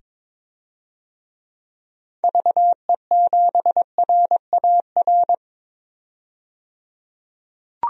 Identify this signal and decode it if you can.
Morse code, heard as a keyed tone, about 22 wpm, 700 Hz. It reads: VE7RAR
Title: DXCC-100 Call Signs - 6 Characters - 22wpm